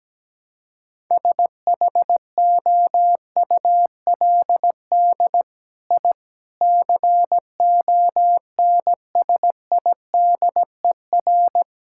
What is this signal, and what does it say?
Morse code, heard as a keyed tone, about 17 wpm, 700 Hz. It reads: SHOULD I CONSIDER
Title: Sentences from Top 1000 Words - Review Part 5 - 17wpm